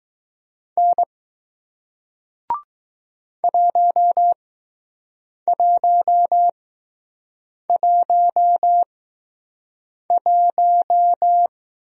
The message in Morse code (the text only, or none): N 1111